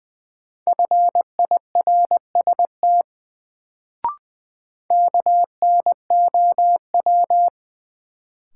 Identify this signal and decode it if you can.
Morse code, heard as a keyed tone, about 20 wpm, 700 Hz. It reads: FIRST KNOW
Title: Sets of 2 Words - New Word Drill - 20wpm